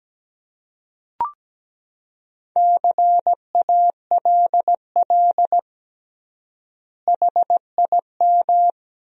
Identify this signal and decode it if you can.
Morse code, heard as a keyed tone, about 17 wpm, 700 Hz. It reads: CALL HIM
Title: 3x Word-Spacing - Sentences from Top 100 Words - 17wpm